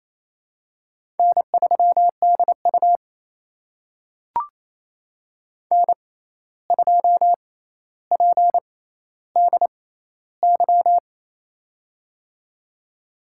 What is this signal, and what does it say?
Morse code, heard as a keyed tone, about 28 wpm, spaced as for 13 wpm, 700 Hz. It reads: N3DU N2PDY
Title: USA Call Signs - 6x Character Spacing - 28wpm